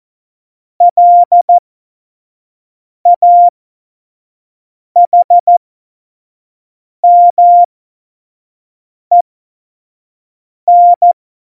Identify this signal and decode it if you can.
Morse code, heard as a keyed tone, about 14 wpm, spaced as for 5 wpm, 700 Hz. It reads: LAHMEN
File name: Morsekurs / Witze / Witz 11 August 2020, 25 F70